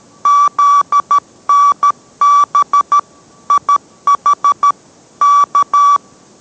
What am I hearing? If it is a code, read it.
Morse code, heard as a keyed tone, about 13 wpm, 1150 Hz. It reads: ZNB IH K